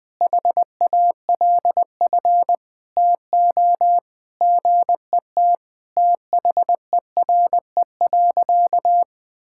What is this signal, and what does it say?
Morse code, heard as a keyed tone, about 21 wpm, 700 Hz. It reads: HALF TO GET THERE.
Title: Sentences with 5-Word Phrases - Spoken Beginning - 20wpm